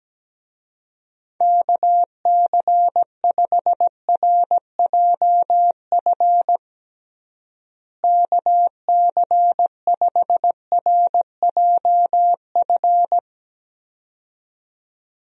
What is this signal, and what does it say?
Morse code, heard as a keyed tone, about 17 wpm, 700 Hz. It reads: KC5RJF KC5RJF
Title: USA Call Signs - 6 Characters - Repeated 2x - 17wpm